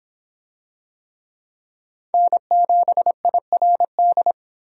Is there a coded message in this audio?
Yes: Morse code from 2.1 to 4.3 s, heard as a keyed tone, about 23 wpm, 700 Hz. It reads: N7IRD